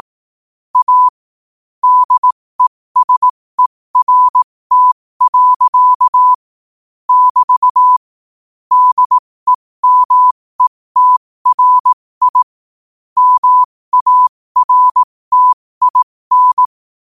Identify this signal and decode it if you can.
Morse code, heard as a keyed tone, about 18 wpm, spaced as for 15 wpm, 1000 Hz. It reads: A DESERT. <BT> DEMETRI MARTIN
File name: qotdfiles / 15wpm